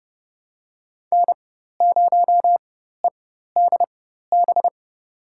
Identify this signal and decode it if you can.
Morse code, heard as a keyed tone, about 32 wpm, spaced as for 14 wpm, 700 Hz. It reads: N0EDB
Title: USA Call Signs - 5 Characters - 4x Character Spacing - 30wpm